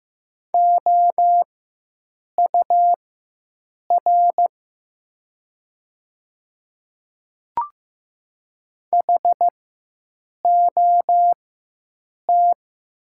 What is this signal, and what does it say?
Morse code, heard as a keyed tone, about 15 wpm, spaced as for 7 wpm, 700 Hz. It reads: OUR HOT
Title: 4x Character Spacing - Top 300 Words - 15wpm